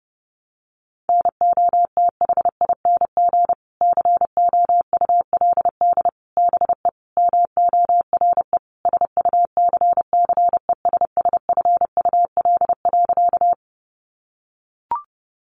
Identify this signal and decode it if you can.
Morse code, heard as a keyed tone, about 30 wpm, 700 Hz. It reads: NOTHING COULD BE MORE SUCCESSFUL.
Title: Sentences from Top 900 Words 30wpm